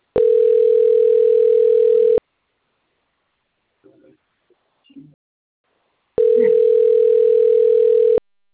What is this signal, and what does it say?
Ringback tone (North America), 440 + 480 Hz, with speech.
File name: rg-1600-unknown- - - .9249